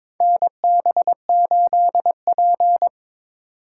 Morse code, heard as a keyed tone, about 22 wpm, 700 Hz. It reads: NB8P
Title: USA Call Signs - 4 Characters - 3x Character Spacing - 22wpm